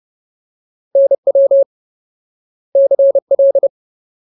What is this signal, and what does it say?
Morse code, heard as a keyed tone, about 30 wpm, 550 Hz. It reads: NW CL
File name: Ultimate Word List / 30WPM Ultimate